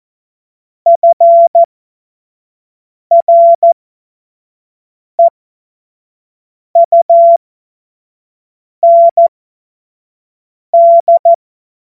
Morse code, heard as a keyed tone, about 14 wpm, spaced as for 5 wpm, 680 Hz. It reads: FREUND
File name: Witz-05-August-2025-25-F70